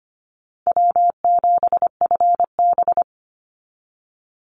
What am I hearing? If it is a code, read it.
Morse code, heard as a keyed tone, about 25 wpm, 700 Hz. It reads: W7FB
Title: Easy Call Signs Repeated 3x - 25wpm